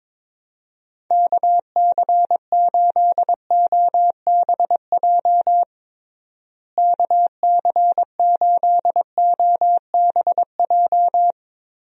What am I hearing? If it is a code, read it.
Morse code, heard as a keyed tone, about 22 wpm, 700 Hz. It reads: KC8OBJ KC8OBJ
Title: USA Call Signs - 6 Characters - Repeated 2x - 22wpm